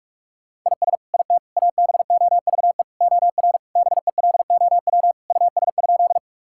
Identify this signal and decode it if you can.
Morse code, heard as a keyed tone, about 46 wpm, 700 Hz. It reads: IS IT ABOVE OR BELOW US?
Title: Sentences from Top 1000 Words - Review Part 5 - 45wpm